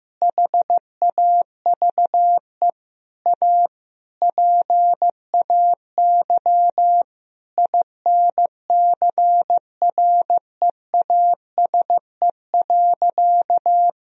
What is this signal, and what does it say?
Morse code, heard as a keyed tone, about 15 wpm, 700 Hz. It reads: HAVE A PAY INCREASE.
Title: Sentences from Top 1000 Words - Review Part 4 - 15wpm